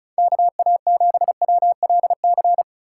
Morse code, heard as a keyed tone, about 44 wpm, 700 Hz. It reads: KA7WLC
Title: Speed Racing - USA 2x3 Call Signs 35wpm